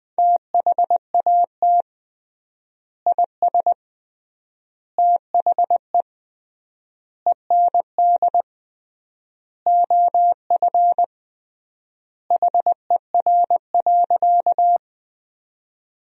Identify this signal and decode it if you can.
Morse code, heard as a keyed tone, about 21 wpm, 700 Hz. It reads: THAT IS THE END OF HER.
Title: 3x Word-Spacing - Sentences from Top 200 Words - 20wpm